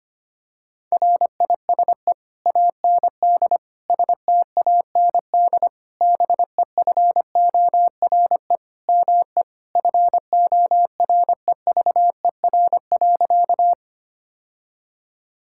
Morse code, heard as a keyed tone, about 25 wpm, 700 Hz. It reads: RISE AND STAND BEFORE ME FOREVER.
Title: Sentences from Top 1600 Words 25wpm